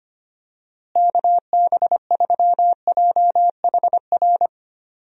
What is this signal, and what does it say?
Morse code, heard as a keyed tone, about 25 wpm, 700 Hz. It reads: KB3JHR
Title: USA Call Signs - 6 Characters - Repeated 2x - 25wpm